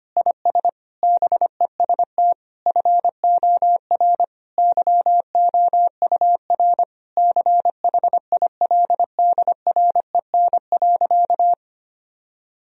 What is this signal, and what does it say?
Morse code, heard as a keyed tone, about 25 wpm, 700 Hz. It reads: IS BEST FOR YOUR CHILDREN.
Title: Sentences from Top 1000 Words - Review Part 4 - 25wpm